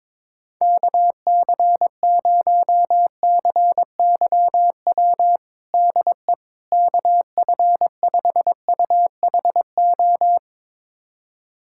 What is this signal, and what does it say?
Morse code, heard as a keyed tone, about 21 wpm, 700 Hz. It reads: KC0CYW DE KF5UHO
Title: USA Callsign DE Callsign - 6 Characters - 22wpm